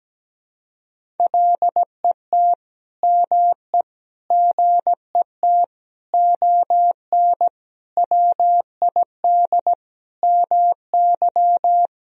Morse code, heard as a keyed tone, about 17 wpm, 700 Hz. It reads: LET ME GET ON WID MY
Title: Sentences from Top 300 Words and Abbreviations 17wpm